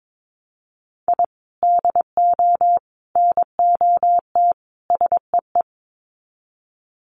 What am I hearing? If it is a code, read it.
Morse code, heard as a keyed tone, about 22 wpm, 700 Hz. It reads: I DO NOT SEE
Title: Sets of 4 Words 22wpm